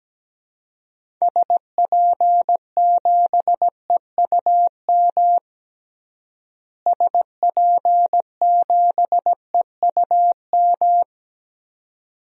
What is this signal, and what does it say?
Morse code, heard as a keyed tone, about 17 wpm, 700 Hz. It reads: SP7EUM SP7EUM
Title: DXCC-100 Call Signs - 6 Characters - Repeated 2x - 17wpm